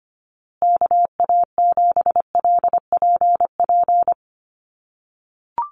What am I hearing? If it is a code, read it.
Morse code, heard as a keyed tone, about 25 wpm, 700 Hz. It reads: KA7LPP
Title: USA Call Signs Repeated 2x 25wpm